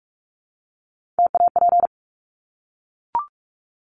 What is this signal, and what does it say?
Morse code, heard as a keyed tone, about 44 wpm, 700 Hz. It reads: TAP
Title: Top 1251-1500 Words 45wpm